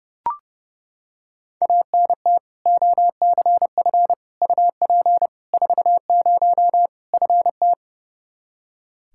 Morse code, heard as a keyed tone, about 30 wpm, 700 Hz. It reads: ANT OCF UP 40 FT
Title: K1USN SST QSO Practice 30wpm